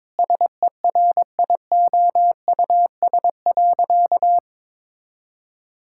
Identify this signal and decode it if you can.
Morse code, heard as a keyed tone, about 21 wpm, 700 Hz. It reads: SERIOUS.
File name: practice / 22wpm / Sentences with 3-Word Phrases - 22wpm